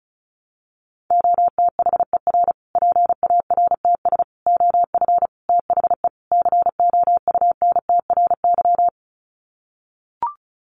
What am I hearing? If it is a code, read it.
Morse code, heard as a keyed tone, about 34 wpm, 700 Hz. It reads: OTHER PARTS OF THE COUNTRY
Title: Sets of 5 Words Encore 35wpm